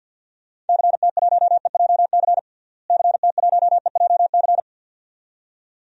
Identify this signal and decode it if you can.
Morse code, heard as a keyed tone, about 50 wpm, 700 Hz. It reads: CT1EJC CT1EJC
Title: DXCC-100 Call Signs - 6 Characters - Repeated 2x - 50wpm